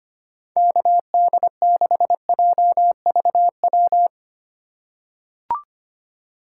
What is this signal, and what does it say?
Morse code, heard as a keyed tone, about 25 wpm, 700 Hz. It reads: KD6JVW
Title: USA Call Signs - 6 Characters - 4x Character Spacing - 25wpm